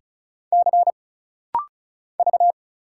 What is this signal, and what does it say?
Morse code, heard as a keyed tone, about 34 wpm, 700 Hz. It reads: C V
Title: Single Letter-Number Alt - Rapid-Fire 35wpm